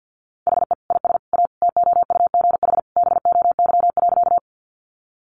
Morse code, heard as a keyed tone, about 50 wpm, 700 Hz. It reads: HE IS A TOUGH BOY.
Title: Sentences from Top 1200 Words 50wpm